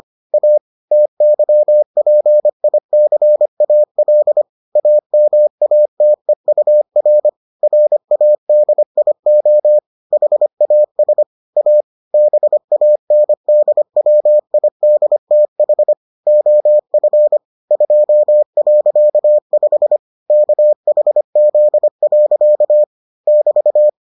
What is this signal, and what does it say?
Morse code, heard as a keyed tone, about 25 wpm, 600 Hz. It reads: A TYPICAL AMATEUR RADIO HAS A BANDWIDTH OF 2.5 KHZ. <BT>